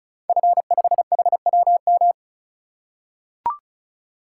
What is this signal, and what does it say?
Morse code, heard as a keyed tone, about 33 wpm, 700 Hz. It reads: F5HWM
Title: Code Groups - 5 Letter:Numbers - 35wpm